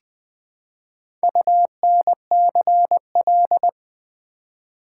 Morse code, heard as a keyed tone, about 20 wpm, 700 Hz. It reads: UNCL
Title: Build A Word - 1 Letter At A Time - 5-Letter Words - 20wpm